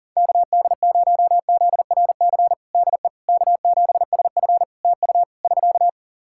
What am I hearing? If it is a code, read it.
Morse code, heard as a keyed tone, about 37 wpm, 700 Hz. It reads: KD0ZRC DE K7SF TU <SK>
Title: Practice QSO - Exchange 4 - Ending - 40wpm